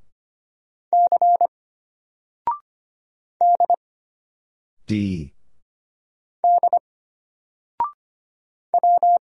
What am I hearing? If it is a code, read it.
Morse code, heard as a keyed tone, about 25 wpm, 700 Hz. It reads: C D D W
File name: Single-Letter-Number-25wpm